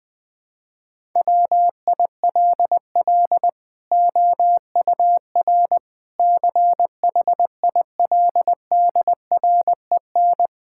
Morse code, heard as a keyed tone, about 20 wpm, 700 Hz. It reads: WILL OUR CHILDREN
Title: Sentences from Top 1000 Words - Review Part 5 - 20wpm